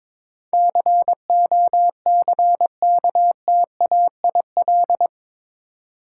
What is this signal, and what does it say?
Morse code, heard as a keyed tone, about 21 wpm, 700 Hz. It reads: COCKTAIL
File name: practice / 22wpm / Eight Letter Words - 22wpm